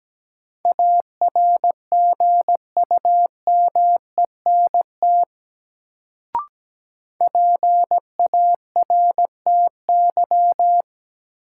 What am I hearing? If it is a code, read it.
Morse code, heard as a keyed tone, about 17 wpm, 700 Hz. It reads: ARGUMENT PARTY
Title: Sentences from Top 1400 Words - New Word Drill - 17wpm